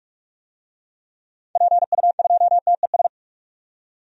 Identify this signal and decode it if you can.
Morse code, heard as a keyed tone, about 44 wpm, 700 Hz. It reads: PU2TES
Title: DXCC-100 Call Signs - 6 Characters - 45wpm